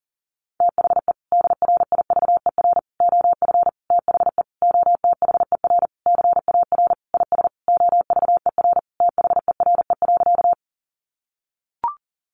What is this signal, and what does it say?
Morse code, heard as a keyed tone, about 40 wpm, 700 Hz. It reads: THE DRIVER OF THE OTHER CAR IS OVER THERE.
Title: Sentences from Top 1000 Words 40wpm